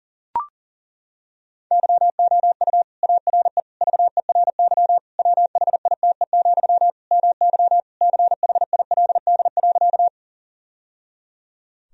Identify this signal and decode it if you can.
Morse code, heard as a keyed tone, about 40 wpm, 700 Hz. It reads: YOU ARE VERY WHITE, MY CHILD.
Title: Sentences from Top 1000 Words - Review Part 1 - 40wpm